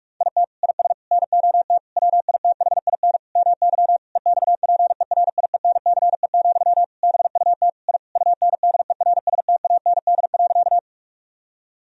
Morse code, heard as a keyed tone, about 41 wpm, 700 Hz. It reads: IT IS NOT WITHIN MY EXPERIENCE, BUT I UNDERSTAND.
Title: Sentences from Top 1000 Words - Review Part 4 - 45wpm